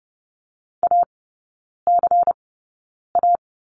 Morse code, heard as a keyed tone, about 28 wpm, spaced as for 9 wpm, 690 Hz. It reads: ACA
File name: Lesson 16e - Letter c - Three Characters 30wpm